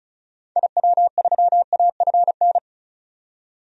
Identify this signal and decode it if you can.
Morse code, heard as a keyed tone, about 36 wpm, 700 Hz. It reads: IW3AFN
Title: DXCC-100 Call Signs - 6 Characters - 35wpm